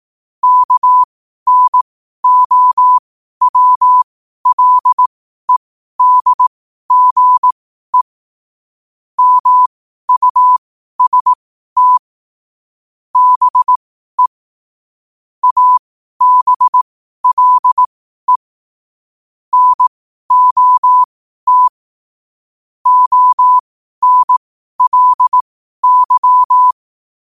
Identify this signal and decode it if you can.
Morse code, heard as a keyed tone, about 18 wpm, spaced as for 12 wpm, 1000 Hz. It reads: KNOWLEDGE MUST BE ABLE NOT ONLY
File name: qotdfiles / 12wpm